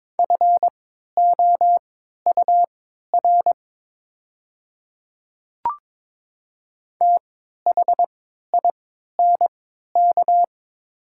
Morse code, heard as a keyed tone, about 23 wpm, spaced as for 13 wpm, 700 Hz. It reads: FOUR THINK
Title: 3x Character Spacing - Top 300 Words - 22wpm